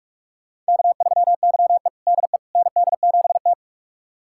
Morse code, heard as a keyed tone, about 41 wpm, 700 Hz. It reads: K3YE DE ND7T
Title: USA Callsign DE Callsign - 4 Characters - 45wpm